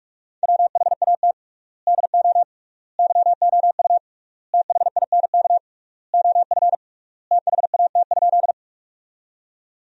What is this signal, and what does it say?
Morse code, heard as a keyed tone, about 41 wpm, 700 Hz. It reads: WHAT DO YOU THINK OF THAT?
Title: 3x Word-Spacing - Sentences from Top 200 Words - 45wpm